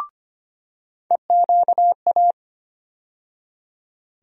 Morse code, heard as a keyed tone, about 25 wpm, 700 Hz. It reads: EQA
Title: USA Call Sign Postfixes - 3 Letters 25wpm